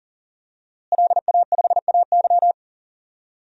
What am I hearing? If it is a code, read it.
Morse code, heard as a keyed tone, about 40 wpm, 700 Hz. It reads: LA5AY
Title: DXCC-100 Call Signs - 5 Characters - Repeated 2x - 40wpm